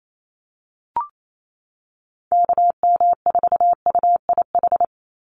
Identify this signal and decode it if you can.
Morse code, heard as a keyed tone, about 28 wpm, 700 Hz. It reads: KM4UIH
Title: USA Call Signs 28wpm